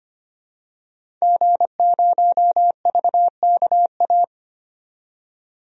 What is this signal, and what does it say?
Morse code, heard as a keyed tone, about 25 wpm, 700 Hz. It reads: G0VKA
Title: DXCC-100 Call Signs - 5 Characters - 25wpm